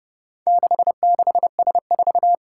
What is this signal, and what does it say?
Morse code, heard as a keyed tone, about 30 wpm, 700 Hz. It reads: 66S4
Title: Four Challenging Letter-Number Combinations 30wpm